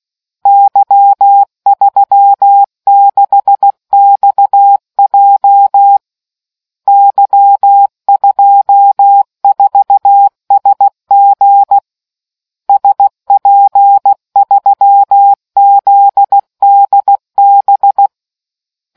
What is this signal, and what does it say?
Morse code, heard as a keyed tone, about 16 wpm, 800 Hz. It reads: Y36XJ Y24SG SP3ZDB